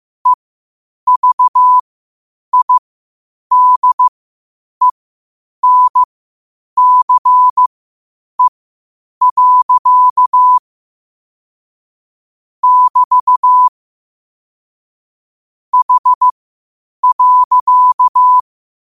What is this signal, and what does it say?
Morse code, heard as a keyed tone, about 15 wpm, spaced as for 8 wpm, 1000 Hz. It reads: EVIDENCE. <BT> H.
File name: qotdfiles / 8wpm